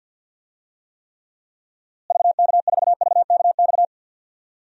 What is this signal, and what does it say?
Morse code, heard as a keyed tone, about 50 wpm, 700 Hz. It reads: VK4VKX
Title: DXCC-100 Call Signs - 6 Characters - Repeated 2x - 50wpm